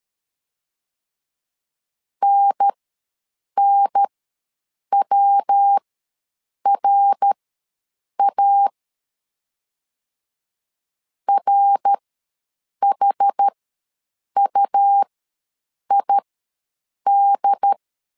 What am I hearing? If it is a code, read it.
Morse code, heard as a keyed tone, about 13 wpm, spaced as for 7 wpm, 800 Hz. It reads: NNWRA RHUID
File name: LEZIONI / 3 MWRO / 3 MWR ws 12wpm 1809